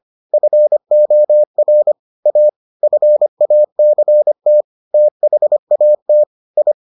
Morse code, heard as a keyed tone, about 25 wpm, 600 Hz. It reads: FOR A FACT THAT I